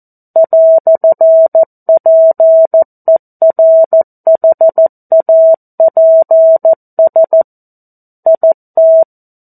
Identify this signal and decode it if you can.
Morse code, heard as a keyed tone, about 14 wpm, 650 Hz. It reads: "PERHAPS IT
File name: 02-Chapter-02 14 WPM